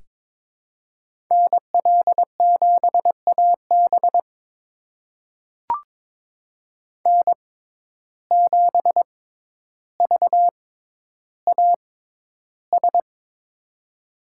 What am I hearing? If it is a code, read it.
Morse code, heard as a keyed tone, about 21 wpm, spaced as for 11 wpm, 700 Hz. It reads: NL7AB N7VAS
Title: USA Call Signs - 5 Characters - 6x Character Spacing - 22wpm